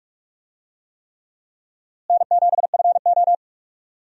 Morse code, heard as a keyed tone, about 44 wpm, 700 Hz. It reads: N7FO
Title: USA Call Signs - 4 Characters - Repeated 3x - 45wpm